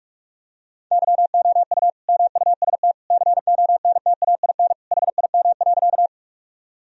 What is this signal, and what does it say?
Morse code, heard as a keyed tone, about 46 wpm, 700 Hz. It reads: YOU MUST CONTAIN HIM.
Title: Sentences from Top 1000 Words - Review Part 2 - 45wpm